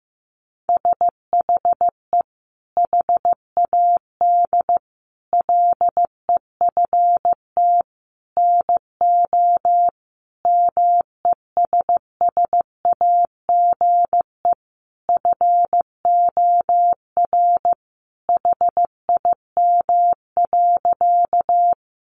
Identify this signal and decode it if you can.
Morse code, heard as a keyed tone, about 14 wpm, 700 Hz. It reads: SHE HAD LEFT NO MESSAGE FOR HIM.
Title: Sentences from Top 1100 Words 15wpm